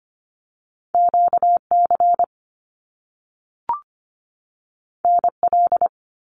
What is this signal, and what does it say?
Morse code, heard as a keyed tone, about 25 wpm, 700 Hz. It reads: QC NL
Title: Canadian Provinces and Territories 25wpm